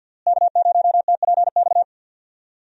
Morse code, heard as a keyed tone, about 50 wpm, 700 Hz. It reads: K0TPX
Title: USA Call Signs - 5 Characters - Repeated 2x - 50wpm